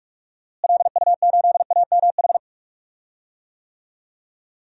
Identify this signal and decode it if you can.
Morse code, heard as a keyed tone, about 46 wpm, 700 Hz. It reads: LU8AMH
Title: DXCC-100 Call Signs - 6 Characters - Repeated 2x - 45wpm